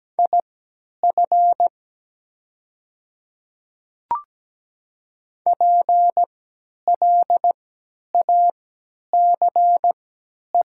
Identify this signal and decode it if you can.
Morse code, heard as a keyed tone, about 17 wpm, spaced as for 10 wpm, 700 Hz. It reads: IF PLACE
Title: 3x Character Spacing - Top 200 Words - 17wpm